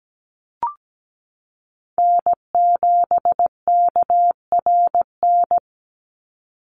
Morse code, heard as a keyed tone, about 17 wpm, 700 Hz. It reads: N7KRN
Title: NA SOTA Chaser Callsigns 17wpm